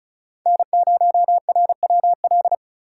Morse code, heard as a keyed tone, about 45 wpm, 700 Hz. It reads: N0RWL